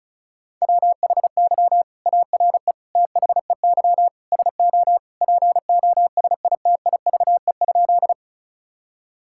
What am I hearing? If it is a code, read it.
Morse code, heard as a keyed tone, about 34 wpm, 700 Hz. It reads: WHY ARE THEY SO POSITIVE?
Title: Sentences from Top 1000 Words - Review Part 5 - 35wpm